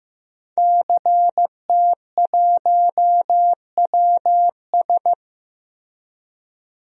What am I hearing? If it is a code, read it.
Morse code, heard as a keyed tone, about 16 wpm, 700 Hz. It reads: CT1WS
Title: DXCC-100 Call Signs - 5 Characters - Repeated 2x - 15wpm